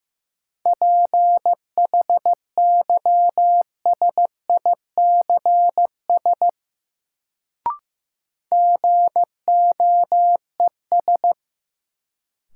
Morse code, heard as a keyed tone, about 15 wpm, 700 Hz. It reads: PHYSICS GOES
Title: Sentences from Top 1500 Words - New Word Drill Alt - 15wpm